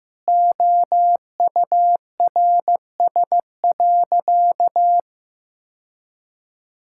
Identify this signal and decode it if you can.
Morse code, heard as a keyed tone, about 16 wpm, 700 Hz. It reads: OURS.
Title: Sentences from Top 1000 Words - Review Part 5 - 15wpm